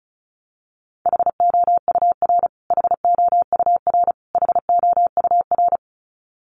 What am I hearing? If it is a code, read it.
Morse code, heard as a keyed tone, about 34 wpm, 700 Hz. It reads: HOUR HOUR HOUR
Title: Top 500 Words Repeated 35wpm